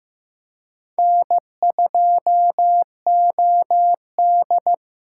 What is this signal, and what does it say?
Morse code, heard as a keyed tone, about 15 wpm, 700 Hz. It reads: N2OD
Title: USA Call Signs - 4 Characters - 2x Character Spacing - 15wpm